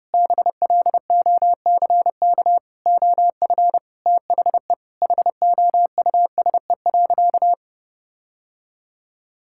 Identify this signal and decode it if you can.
Morse code, heard as a keyed tone, about 30 wpm, 700 Hz. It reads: BLOCK OF THE HOUSE.
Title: Sentences from Top 1600 Words 30wpm